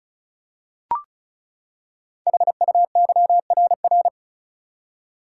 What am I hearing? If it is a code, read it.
Morse code, heard as a keyed tone, about 36 wpm, 700 Hz. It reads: HUYRR
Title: Code Groups - 5 Letters - 35wpm